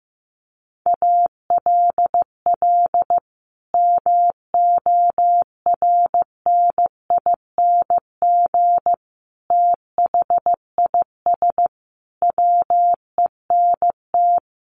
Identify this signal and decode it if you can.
Morse code, heard as a keyed tone, about 15 wpm, 700 Hz. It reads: ALL MORNING THIS WENT
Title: Sentences from Top 1500 Words 15wpm